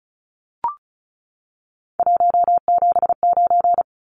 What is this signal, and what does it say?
Morse code, heard as a keyed tone, about 34 wpm, 700 Hz. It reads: 179
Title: Numbers - 3-Digit 35wpm